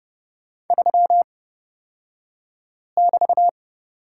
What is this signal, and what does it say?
Morse code, heard as a keyed tone, about 30 wpm, spaced as for 5 wpm, 700 Hz. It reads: 3<BT>
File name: Lesson 36d - BT - Two Characters 30wpm